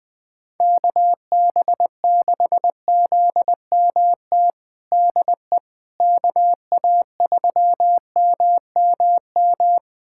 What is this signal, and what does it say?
Morse code, heard as a keyed tone, about 20 wpm, 700 Hz. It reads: KB6ZMT DE KA3MMM
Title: USA Callsign DE Callsign 20wpm